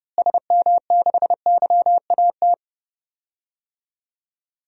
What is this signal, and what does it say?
Morse code, heard as a keyed tone, about 30 wpm, 700 Hz. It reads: SM6YAT
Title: DXCC-100 Call Signs - 6 Characters - Repeated 2x - 30wpm